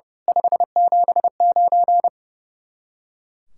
Morse code, heard as a keyed tone, about 30 wpm, 700 Hz. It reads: 579
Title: Speed-Racing Common QSO Phrases Encore - 30wpm